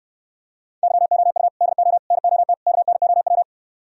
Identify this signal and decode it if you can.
Morse code, heard as a keyed tone, about 69 wpm, 700 Hz. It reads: YOU DO NOT KNOW